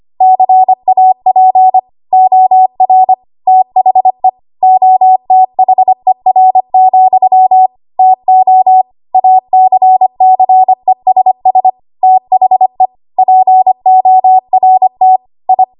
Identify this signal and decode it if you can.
Morse code, heard as a keyed tone, about 25 wpm, 750 Hz. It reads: CAP OR THE OTHER, TO ACCESS THE PORT I
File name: Archive / 25 WPM / 25WPM